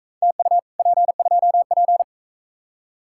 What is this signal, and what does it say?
Morse code, heard as a keyed tone, about 44 wpm, 700 Hz. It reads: TU P2P